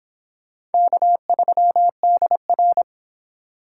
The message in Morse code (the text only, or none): K3DR